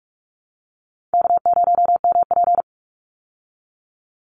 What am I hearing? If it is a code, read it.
Morse code, heard as a keyed tone, about 44 wpm, 700 Hz. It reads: K0MP
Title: Call Signs Repeated 2x - 45wpm